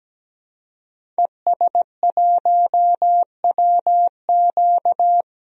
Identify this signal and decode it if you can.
Morse code, heard as a keyed tone, about 17 wpm, 700 Hz. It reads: ES1WQ
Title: DXCC-100 Call Signs - 5 Characters - 17wpm